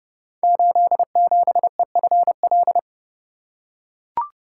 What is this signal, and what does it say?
Morse code, heard as a keyed tone, about 30 wpm, 700 Hz. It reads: 87EFL
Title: Code Groups - 5 Letter:Numbers - 30wpm